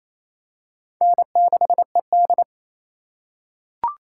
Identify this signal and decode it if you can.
Morse code, heard as a keyed tone, about 28 wpm, 700 Hz. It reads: N6ED
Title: USA Call Signs - 4 Characters - 3x Character Spacing - 28wpm